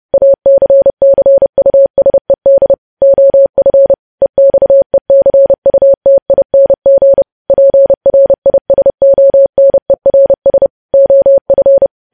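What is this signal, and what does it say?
Morse code, heard as a keyed tone, about 30 wpm, 550 Hz. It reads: ACCUSED OF EXECUTING PRISONERS OF
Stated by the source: Morse code news headlines sent at 30 WPM.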